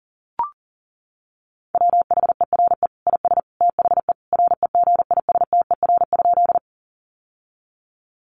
Morse code, heard as a keyed tone, about 40 wpm, 700 Hz. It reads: WHERE IS THE REGISTER?
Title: Sentences from Top 1200 Words 40wpm